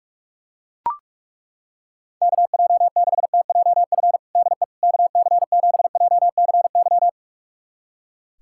Morse code, heard as a keyed tone, about 41 wpm, 700 Hz. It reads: KJ6TJF DE KC7JCY
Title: USA Callsign DE Callsign - 6 Characters - 45wpm